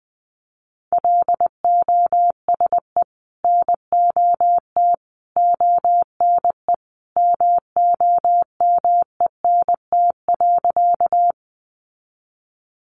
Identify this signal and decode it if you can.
Morse code, heard as a keyed tone, about 20 wpm, 700 Hz. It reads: LOSE NOT ONE MOMENT.
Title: Sentences from Top 1100 Words 20wpm